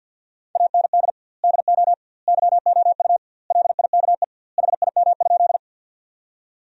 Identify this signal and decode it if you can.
Morse code, heard as a keyed tone, about 53 wpm, 700 Hz. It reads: AND DO YOU LIKE HIM?
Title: 2x Word-Spacing - Sentences from Top 100 Words - 50wpm